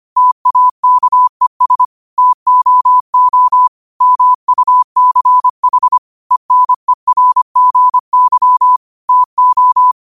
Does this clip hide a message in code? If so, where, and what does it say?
Yes: Morse code from 0.2 to 9.9 s, heard as a keyed tone, about 25 wpm, 1000 Hz. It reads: TAKES TOO MUCH ENERGY TO